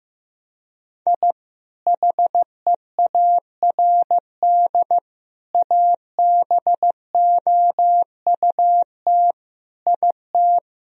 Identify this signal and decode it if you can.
Morse code, heard as a keyed tone, about 16 wpm, 700 Hz. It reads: I HEARD ABOUT IT
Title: Sentences from Top 1000 Words - Review Part 1 - 15wpm